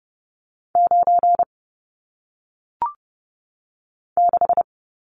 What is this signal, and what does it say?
Morse code, heard as a keyed tone, about 30 wpm, 700 Hz. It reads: 9 6
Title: Numbers - 1-Digit 30wpm